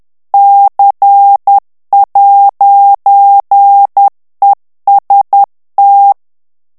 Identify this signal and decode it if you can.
Morse code, heard as a keyed tone, about 10 wpm, 800 Hz. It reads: C'EST
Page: Français - texte en clair 1 - vitesse : 10 mots minute